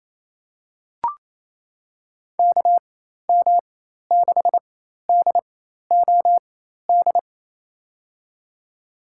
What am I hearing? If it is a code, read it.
Morse code, heard as a keyed tone, about 28 wpm, spaced as for 13 wpm, 700 Hz. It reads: KM6DOD
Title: USA Call Signs - 4x Character Spacing - 28wpm